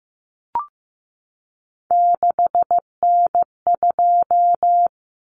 Morse code, heard as a keyed tone, about 15 wpm, 700 Hz. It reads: 6N2
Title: Common Cut Numbers - 3-Digit 15wpm